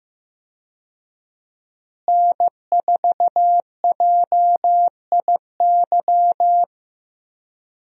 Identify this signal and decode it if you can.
Morse code, heard as a keyed tone, about 15 wpm, 700 Hz. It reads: N4JIY
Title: USA Call Signs - 5 Characters - Repeated 3x - 15wpm